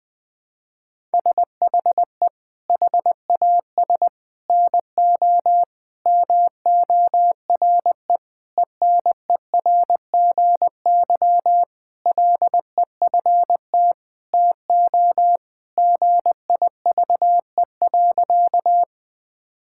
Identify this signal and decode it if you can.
Morse code, heard as a keyed tone, about 20 wpm, 700 Hz. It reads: SHE HAS NO MORE ENERGY LEFT TO GIVE.
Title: Sentences from Top 1000 Words - Review Part 6 - 20wpm